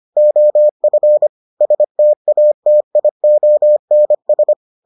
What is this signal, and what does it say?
Morse code, heard as a keyed tone, about 25 wpm, 600 Hz. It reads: OF STATIONS